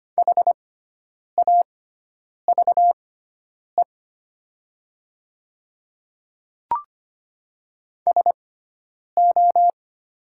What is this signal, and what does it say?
Morse code, heard as a keyed tone, about 25 wpm, spaced as for 9 wpm, 700 Hz. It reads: HAVE SO
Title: 6x Character Spacing - Top 100 Words - 25wpm